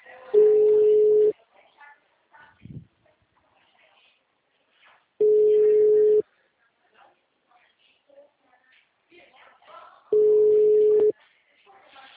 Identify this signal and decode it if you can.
Ringback tone (Europe, 425 Hz), with speech.
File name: out- -2025- - - .19090